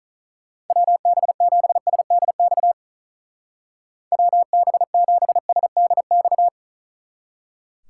Morse code, heard as a keyed tone, about 47 wpm, 700 Hz. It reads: WB7SDX WB7SDX